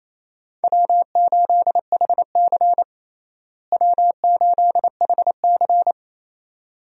Morse code, heard as a keyed tone, about 28 wpm, 700 Hz. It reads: W8HC W8HC
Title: USA Call Signs - 4 Characters - Repeated 2x - 28wpm